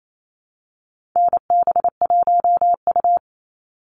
Morse code, heard as a keyed tone, about 28 wpm, 700 Hz. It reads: NB1U
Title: Call Signs Repeated 3x - 28wpm